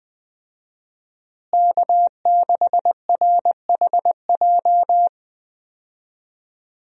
Morse code, heard as a keyed tone, about 20 wpm, 700 Hz. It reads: K6RHJ
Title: USA Call Signs - 5 Characters - Repeated 2x - 20wpm